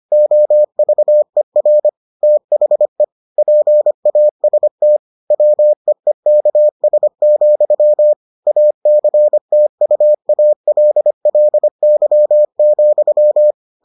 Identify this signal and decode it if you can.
Morse code, heard as a keyed tone, about 25 wpm, 600 Hz. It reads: OVER THE PAST WEEKS, ACTUALLY,